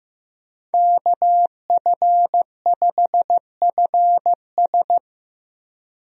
Morse code, heard as a keyed tone, about 15 wpm, 700 Hz. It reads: KF5FS
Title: USA Call Signs - 5 Characters - 15wpm